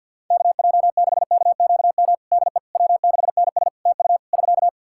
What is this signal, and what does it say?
Morse code, heard as a keyed tone, about 50 wpm, 700 Hz. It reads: KJ6KQM DE W6NS TU <SK>
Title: Practice QSO - Exchange 4 - Ending - 50wpm